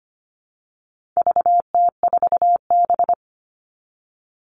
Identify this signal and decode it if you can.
Morse code, heard as a keyed tone, about 25 wpm, 700 Hz. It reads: VT4B
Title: All Cut Numbers - 4-Digit 25wpm